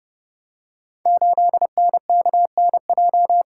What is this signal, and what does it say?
Morse code, heard as a keyed tone, about 30 wpm, 700 Hz. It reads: 8NKNJ
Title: Code Groups - 5 Letter:Numbers - 30wpm